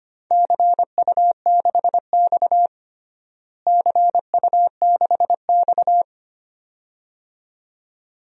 Morse code, heard as a keyed tone, about 25 wpm, 700 Hz. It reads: CU6X CU6X
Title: DXCC-100 Call Signs - 4 Characters - Repeated 2x - 25wpm